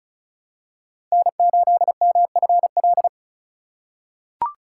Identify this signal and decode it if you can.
Morse code, heard as a keyed tone, about 36 wpm, 700 Hz. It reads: N8MFL
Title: USA Call Signs - 5 Characters - 2x Character Spacing - 35wpm